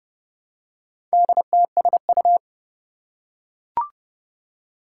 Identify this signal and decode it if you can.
Morse code, heard as a keyed tone, about 30 wpm, 700 Hz. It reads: DTSU
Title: Four Challenging Letter-Number Combinations 30wpm